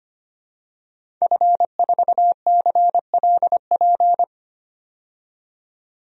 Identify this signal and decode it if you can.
Morse code, heard as a keyed tone, about 25 wpm, 700 Hz. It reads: F4CLP
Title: DXCC-100 Call Signs - 5 Characters - 25wpm